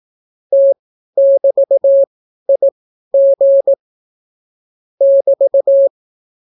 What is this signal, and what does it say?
Morse code, heard as a keyed tone, about 18 wpm, spaced as for 12 wpm, 550 Hz. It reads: T<BT>IG <BT>